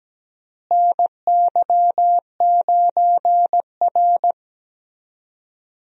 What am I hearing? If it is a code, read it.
Morse code, heard as a keyed tone, about 17 wpm, 700 Hz. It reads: NY9R
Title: USA Call Signs - 4 Characters - Repeated 3x - 17wpm